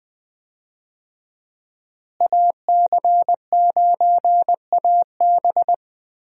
Morse code, heard as a keyed tone, about 20 wpm, 700 Hz. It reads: AC9AB
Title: USA Call Signs - 5 Characters - Repeated 2x - 20wpm